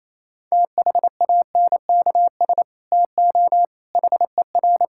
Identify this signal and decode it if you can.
Morse code, heard as a keyed tone, about 28 wpm, 700 Hz. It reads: THANKS TO HER
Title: Sentences from Top 1000 Words - Review Part 5 - 28wpm